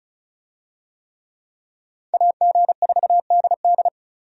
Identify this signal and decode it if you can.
Morse code, heard as a keyed tone, about 34 wpm, 700 Hz. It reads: AG4DD
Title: USA Call Signs - 5 Characters - Repeated 3x - 35wpm